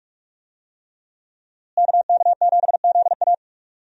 Morse code, heard as a keyed tone, about 46 wpm, 700 Hz. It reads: KK7ZA
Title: USA Call Signs - 5 Characters - Repeated 3x - 45wpm